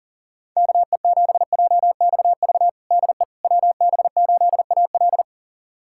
Morse code, heard as a keyed tone, about 37 wpm, 700 Hz. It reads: KE7JXV DE WB8AL
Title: USA Callsign DE Callsign 40wpm